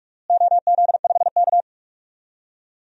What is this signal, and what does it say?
Morse code, heard as a keyed tone, about 42 wpm, 700 Hz. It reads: OZ5K
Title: DXCC-100 Call Signs - 4 Characters - Repeated 2x - 45wpm